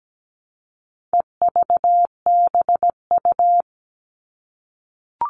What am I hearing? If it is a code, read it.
Morse code, heard as a keyed tone, about 17 wpm, 700 Hz. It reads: EVBU
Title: All Cut Numbers - 4-Digit 17wpm